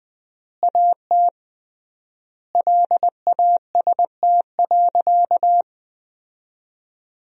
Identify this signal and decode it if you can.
Morse code, heard as a keyed tone, about 20 wpm, 700 Hz. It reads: AT LAST.
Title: 3x Word-Spacing - Sentences from Top 300 Words - 20wpm